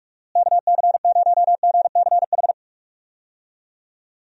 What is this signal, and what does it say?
Morse code, heard as a keyed tone, about 43 wpm, 700 Hz. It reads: KC0GCH
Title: USA Call Signs - 6 Characters - Repeated 3x - 45wpm